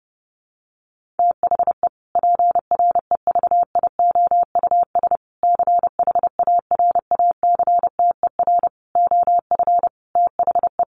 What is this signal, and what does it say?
Morse code, heard as a keyed tone, about 28 wpm, 700 Hz. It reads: THE PREVIOUS CHARACTER OF THE
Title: Sentences from Top 1500 Words 30wpm